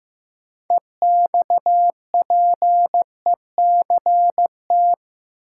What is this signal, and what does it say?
Morse code, heard as a keyed tone, about 15 wpm, 700 Hz. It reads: EXPECT
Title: Sentences from Top 1100 Words - New Word Drill Alt - 15wpm